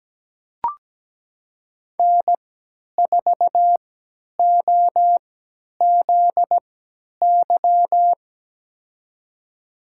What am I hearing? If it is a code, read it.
Morse code, heard as a keyed tone, about 17 wpm, spaced as for 10 wpm, 700 Hz. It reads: N4OZY
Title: USA Call Signs - 5 Characters - 3x Character Spacing - 17wpm